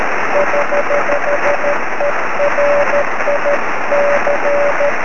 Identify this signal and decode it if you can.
Morse code, heard as a keyed tone, about 13 wpm, 590 Hz, with noise. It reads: <HH> ERIC
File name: sm7lcb / 20 23cm / 1615 pa3n 23cm